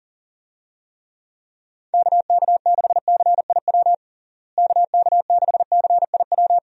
Morse code, heard as a keyed tone, about 40 wpm, 700 Hz. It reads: KK6CIW KK6CIW
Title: USA Call Signs - 6 Characters - Repeated 2x - 40wpm